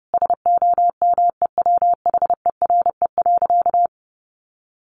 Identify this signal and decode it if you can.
Morse code, heard as a keyed tone, about 30 wpm, 700 Hz. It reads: SOMEWHERE.
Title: Sentences from Top 1400 Words 30wpm